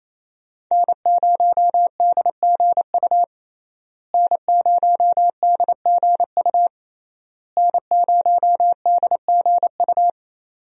Morse code, heard as a keyed tone, about 28 wpm, 700 Hz. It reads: N0DGU N0DGU N0DGU
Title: USA Call Signs - 5 Characters - Repeated 3x - 28wpm